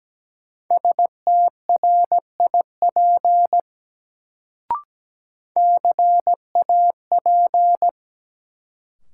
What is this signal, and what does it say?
Morse code, heard as a keyed tone, about 17 wpm, 700 Hz. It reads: STRIP CAP
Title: Sentences from Top 1600 Words - New Word Drill - 17wpm